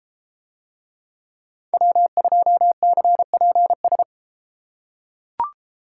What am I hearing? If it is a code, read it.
Morse code, heard as a keyed tone, about 32 wpm, 700 Hz. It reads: W2CPS